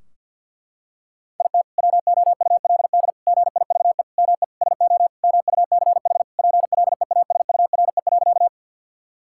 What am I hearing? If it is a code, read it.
Morse code, heard as a keyed tone, about 50 wpm, 700 Hz. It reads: IT WOULD GIVE ME SO MUCH PLEASURE.
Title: Sentences from Top 1600 Words 50wpm